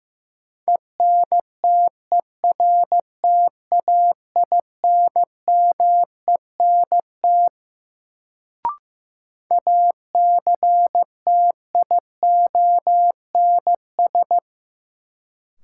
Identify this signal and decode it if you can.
Morse code, heard as a keyed tone, about 15 wpm, 700 Hz. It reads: ENTERTAINMENT ACTIONS
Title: Sentences from Top 1000 Words - Review Part 6 - New Word Drill - 15wpm